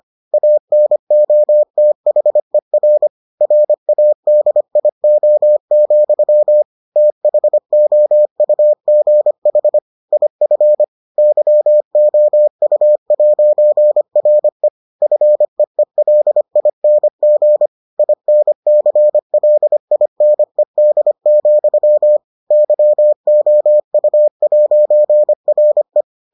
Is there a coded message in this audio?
Yes: Morse code from 0.3 to 26.0 s, heard as a keyed tone, about 25 wpm, 600 Hz. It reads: ANOTHER RADIO, THOUGH IF YOU'RE FEELING INCLINED, YOU'RE